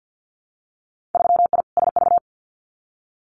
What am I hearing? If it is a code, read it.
Morse code, heard as a keyed tone, about 50 wpm, 700 Hz. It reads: 3I SV
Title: Winter Field Day Exchanges Encore 50wpm